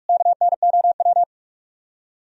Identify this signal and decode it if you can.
Morse code, heard as a keyed tone, about 59 wpm, spaced as for 52 wpm, 700 Hz. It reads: KNOW
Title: Speed-Racing - Sets of 4 Words 45wpm